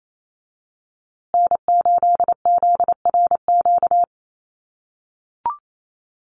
Morse code, heard as a keyed tone, about 28 wpm, 700 Hz. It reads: N8ZRQ
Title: USA Call Signs Repeated 3x 28wpm